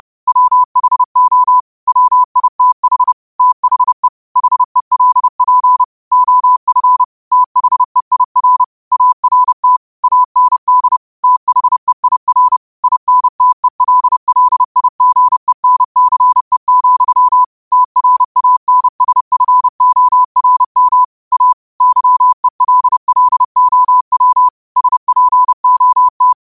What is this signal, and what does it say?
Morse code, heard as a keyed tone, about 30 wpm, 1000 Hz. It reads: WHO WITH THE HELP OF THEIR ART AND THEIR INTELLIGENCE, TRANSFORM A YELLOW SPOT
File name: qotdfiles / 30wpm